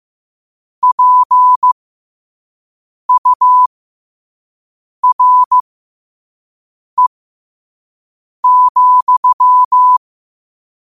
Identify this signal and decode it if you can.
Morse code, heard as a keyed tone, about 15 wpm, spaced as for 5 wpm, 1000 Hz. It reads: PURE,
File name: qotdfiles / 5wpm